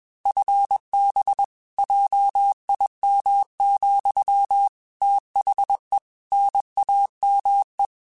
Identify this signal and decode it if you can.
Morse code, heard as a keyed tone, about 21 wpm, 800 Hz. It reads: FB JIM, THE NAME
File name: QSOs / QSOs-mp3s / WD0BGP 1@20